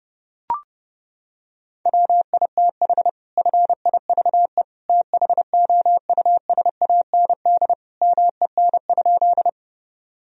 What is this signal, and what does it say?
Morse code, heard as a keyed tone, about 30 wpm, 700 Hz. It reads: WITH FIVE THOUSAND MEN?
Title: Sentences from Top 1000 Words - Review Part 3 - 30wpm